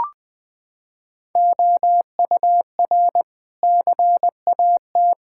Morse code, heard as a keyed tone, about 20 wpm, 700 Hz. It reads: OUR CAT
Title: Sentences with 5-Word Phrases - 20wpm